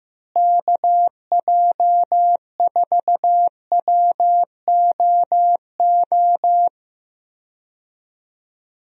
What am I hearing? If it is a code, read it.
Morse code, heard as a keyed tone, about 15 wpm, 700 Hz. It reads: KJ4WOO
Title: USA Call Signs - 6 Characters - Repeated 3x - 15wpm